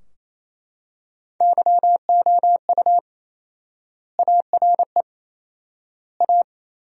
Morse code, heard as a keyed tone, about 28 wpm, 700 Hz. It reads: YOU ARE A
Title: 4x Word-Spacing - Sentences from Top 200 Words - 28wpm